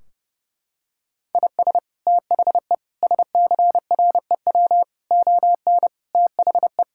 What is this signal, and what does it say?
Morse code, heard as a keyed tone, about 32 wpm, 700 Hz. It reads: IS THE SCREW ON THE
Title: Sentences from Top 1600 Words 30wpm